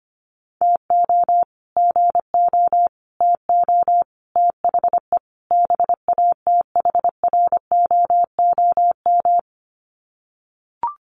Morse code, heard as a keyed tone, about 25 wpm, 700 Hz. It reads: TO GO TO THE BATHROOM
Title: Sets of 5 Words Encore 25wpm